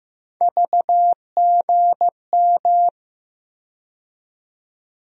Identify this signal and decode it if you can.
Morse code, heard as a keyed tone, about 15 wpm, 700 Hz. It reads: VGM
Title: USA Call Sign Postfixes - 3 Letters 15wpm